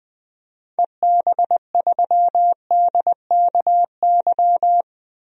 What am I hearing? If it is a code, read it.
Morse code, heard as a keyed tone, about 20 wpm, 700 Hz. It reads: EB3DKY
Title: DXCC-100 Call Signs - 6 Characters - 20wpm